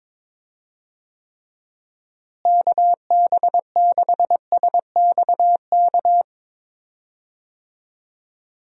Morse code, heard as a keyed tone, about 22 wpm, 700 Hz. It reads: KB6SXK
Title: USA Call Signs - 6 Characters - Repeated 2x - 22wpm